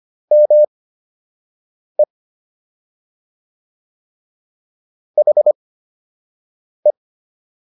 Morse code, heard as a keyed tone, about 25 wpm, spaced as for 6 wpm, 600 Hz. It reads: ME HE